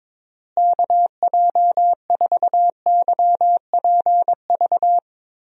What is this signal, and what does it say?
Morse code, heard as a keyed tone, about 22 wpm, 700 Hz. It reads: KJ4YPV
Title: USA Call Signs - 6 Characters - Repeated 3x - 22wpm